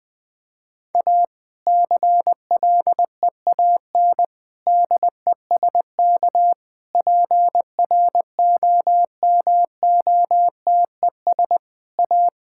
Morse code, heard as a keyed tone, about 20 wpm, 700 Hz. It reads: A CLEAN DESK PROMOTES A
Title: Difficult Sentences with Hints - 20wpm